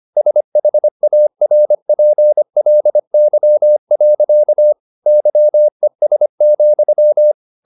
Morse code, heard as a keyed tone, about 25 wpm, 600 Hz. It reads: SHARPLY. YES,
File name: the old man and the sea part 1 25wpm